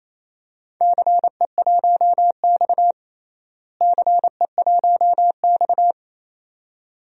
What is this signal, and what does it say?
Morse code, heard as a keyed tone, about 28 wpm, 700 Hz. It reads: CE1X CE1X
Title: DXCC-100 Call Signs - 4 Characters - Repeated 2x - 28wpm